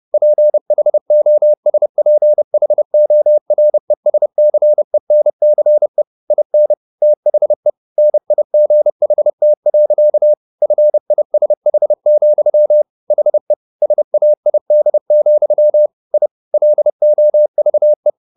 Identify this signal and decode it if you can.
Morse code, heard as a keyed tone, about 30 wpm, 600 Hz. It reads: PHOSPHORESCENCE IN THE NIGHT. FISH, HE SAID, I LOVE
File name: the old man and the sea / the old man and the sea 1 30wpm